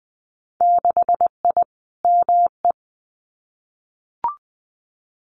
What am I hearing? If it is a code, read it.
Morse code, heard as a keyed tone, about 20 wpm, 700 Hz. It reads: 6I ME
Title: Winter Field Day Exchanges Encore 20wpm